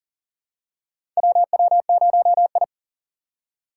Morse code, heard as a keyed tone, about 38 wpm, 690 Hz. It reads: WW0I
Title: USA Call Signs - 4 Characters - 2x Character Spacing - 40wpm